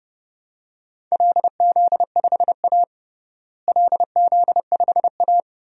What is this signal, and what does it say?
Morse code, heard as a keyed tone, about 30 wpm, 700 Hz. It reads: LZ5A LZ5A
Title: DXCC-100 Call Signs - 4 Characters - Repeated 2x - 30wpm